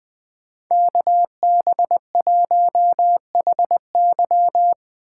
Morse code, heard as a keyed tone, about 25 wpm, 700 Hz. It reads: KB1HY
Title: Speed Racing - USA 2x2 Call Signs 20wpm